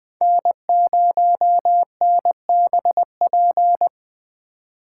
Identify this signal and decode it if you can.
Morse code, heard as a keyed tone, about 24 wpm, 700 Hz. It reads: N0NBP
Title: Speed Racing - USA 1x3 Call Signs 20wpm